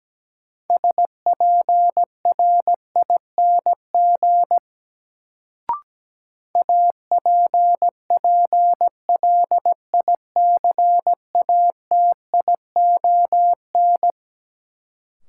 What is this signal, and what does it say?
Morse code, heard as a keyed tone, about 17 wpm, 700 Hz. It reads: SPRING APPLICATION
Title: Sentences from Top 1200 Words - New Word Drill - 17wpm